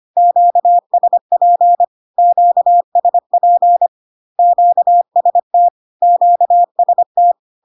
Morse code, heard as a keyed tone, about 25 wpm, 700 Hz. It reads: QSP QSP QST QST
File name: Q-codes most common repeated 3 times 25 25wpm